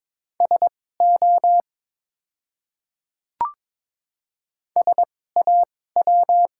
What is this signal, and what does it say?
Morse code, heard as a keyed tone, about 22 wpm, spaced as for 16 wpm, 700 Hz. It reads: SO SAW
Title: 2x Character Spacing - Top 300 Words - 22wpm